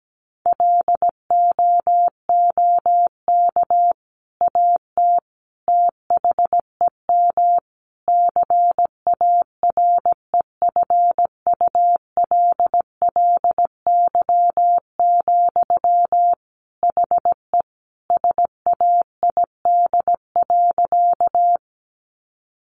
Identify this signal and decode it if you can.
Morse code, heard as a keyed tone, about 17 wpm, 700 Hz. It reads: LOOK AT THEM CAREFULLY, HE SAID.
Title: Sentences from Top 1300 Words 17wpm